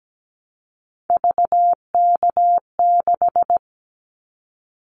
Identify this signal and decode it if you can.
Morse code, heard as a keyed tone, about 17 wpm, 700 Hz. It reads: VK6
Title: Callsign Prefixes for Top 100 DXCC - 17wpm